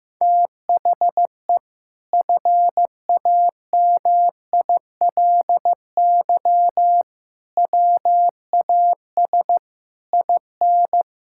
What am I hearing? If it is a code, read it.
Morse code, heard as a keyed tone, about 15 wpm, 700 Hz. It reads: THE FAMILY WAS IN
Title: Sentences from Top 1000 Words - Review Part 6 - 15wpm